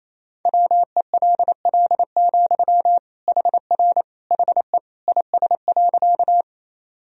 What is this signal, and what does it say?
Morse code, heard as a keyed tone, about 27 wpm, 700 Hz. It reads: WELL, HR HE IS.
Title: Sentences from Top 200 Words and Abbreviations 28wpm